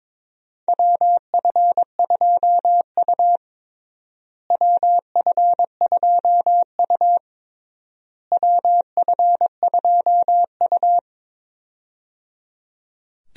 Morse code, heard as a keyed tone, about 21 wpm, 700 Hz. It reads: WF2U WF2U WF2U
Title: USA Call Signs - 4 Characters - Repeated 3x - 22wpm